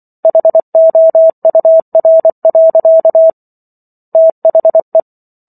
Morse code, heard as a keyed tone, about 24 wpm, 650 Hz. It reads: HOUR. THE
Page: War of the Worlds - 15-Chapter 15 - 24 WPM